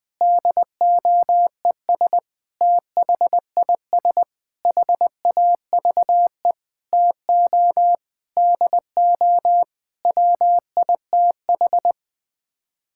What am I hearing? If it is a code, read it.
Morse code, heard as a keyed tone, about 20 wpm, 700 Hz. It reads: DOES THIS HAVE TO DO WITH
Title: Sets of 8 Words - 20wpm